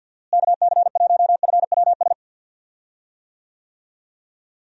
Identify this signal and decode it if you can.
Morse code, heard as a keyed tone, about 52 wpm, 700 Hz. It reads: KC1FWS
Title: USA Call Signs - 6 Characters - Repeated 2x - 50wpm